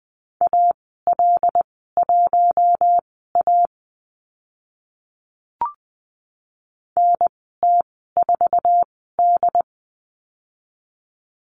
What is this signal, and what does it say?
Morse code, heard as a keyed tone, about 20 wpm, spaced as for 14 wpm, 700 Hz. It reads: AL1A NT4D
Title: 088 Lesson 15g - Number 5 - Call Signs 20wpm